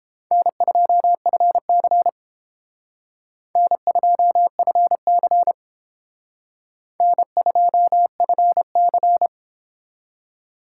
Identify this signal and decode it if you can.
Morse code, heard as a keyed tone, about 29 wpm, 700 Hz. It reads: N2FC N2FC N2FC